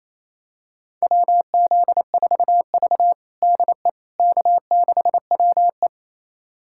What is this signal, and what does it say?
Morse code, heard as a keyed tone, about 28 wpm, 700 Hz. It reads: WZ4V DE K6WE
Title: USA Callsign DE Callsign - 4 Characters - 28wpm